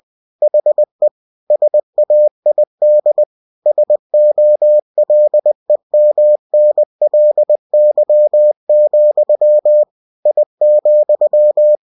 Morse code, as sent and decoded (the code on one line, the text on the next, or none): .... / ... .- .. -.. / ... --- .-.. -- -. .-.. -.-- --..-- / .. --..--
HE SAID SOLEMNLY, I,